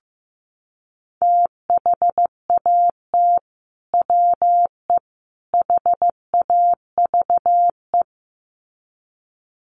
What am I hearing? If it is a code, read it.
Morse code, heard as a keyed tone, about 14 wpm, 700 Hz. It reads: THAT WE HAVE
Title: Sets of 3 Words 15wpm